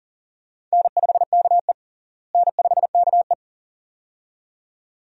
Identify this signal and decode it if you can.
Morse code, heard as a keyed tone, about 40 wpm, 700 Hz. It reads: N5KE N5KE
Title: USA Call Signs - 4 Characters - Repeated 2x - 40wpm